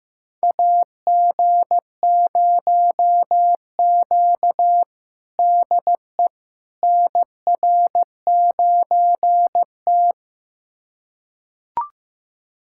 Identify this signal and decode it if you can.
Morse code, heard as a keyed tone, about 15 wpm, 700 Hz. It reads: AG0Q DE NR9T
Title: USA Callsign DE Callsign - 4 Characters - 15wpm